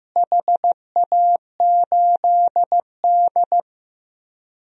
Morse code, heard as a keyed tone, about 16 wpm, 700 Hz. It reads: HA8D
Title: DXCC-100 Call Signs - 4 Characters - Repeated 3x - 15wpm